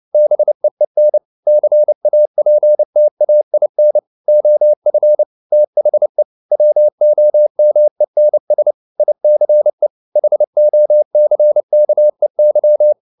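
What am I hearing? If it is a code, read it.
Morse code, heard as a keyed tone, about 29 wpm, 600 Hz. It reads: BEEN CAPTAIN OF THE WOMENS ICE HOCKEY
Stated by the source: Morse code at 29 wpm, sending Ken Follett's novel Never in English, from CW med Gnister.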